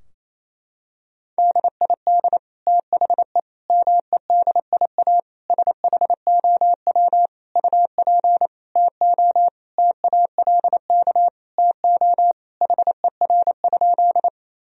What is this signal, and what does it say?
Morse code, heard as a keyed tone, about 28 wpm, 700 Hz. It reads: DID THE MEDIA SHOW UP TO TALK TO HER?
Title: Sentences from Top 1000 Words - Review Part 4 - 28wpm